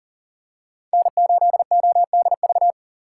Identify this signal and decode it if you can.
Morse code, heard as a keyed tone, about 37 wpm, 700 Hz. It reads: N8ODV
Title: USA Call Signs - 5 Characters - 6x Character Spacing - 40wpm